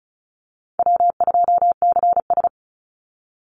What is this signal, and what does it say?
Morse code, heard as a keyed tone, about 36 wpm, 700 Hz. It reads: W2CS
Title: Call Signs - 35wpm